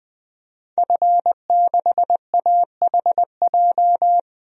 Morse code, heard as a keyed tone, about 20 wpm, 700 Hz. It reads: F6AHJ
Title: DXCC-100 Call Signs - 5 Characters - 20wpm